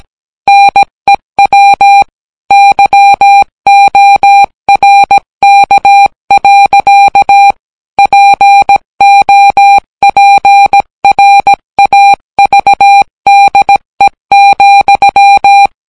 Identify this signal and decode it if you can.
Morse code, heard as a keyed tone, about 17 wpm, 800 Hz. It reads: NEW YORK. POPRAVDE,